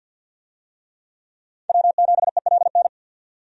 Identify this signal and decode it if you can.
Morse code, heard as a keyed tone, about 51 wpm, 700 Hz. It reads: W7ELN
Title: USA Call Signs - 5 Characters - 50wpm